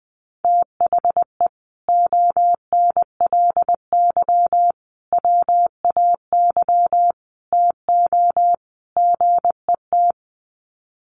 Morse code, heard as a keyed tone, about 20 wpm, 700 Hz. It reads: THE ONLY WAY TO GET
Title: Sets of 5 Words Encore2 20wpm